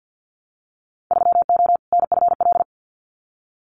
Morse code, heard as a keyed tone, about 50 wpm, 700 Hz. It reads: 3O NFL
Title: Winter Field Day Exchanges 50wpm